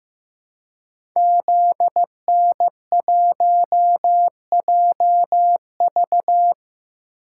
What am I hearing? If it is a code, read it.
Morse code, heard as a keyed tone, about 15 wpm, 700 Hz. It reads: ZN1JV
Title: Code Groups - 5 Letter:Numbers - 15wpm